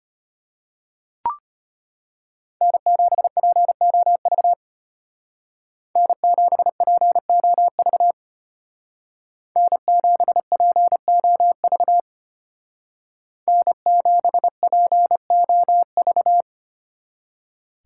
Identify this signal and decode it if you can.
Morse code, heard as a keyed tone, about 31 wpm, 700 Hz. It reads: N7POV N7POV N7POV N7POV
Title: Speed Racing - USA Call Signs 25wpm